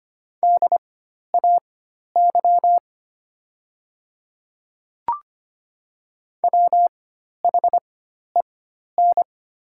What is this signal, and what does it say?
Morse code, heard as a keyed tone, about 25 wpm, spaced as for 12 wpm, 700 Hz. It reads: DAY WHEN
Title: 4x Character Spacing - Top 100 Words - 25wpm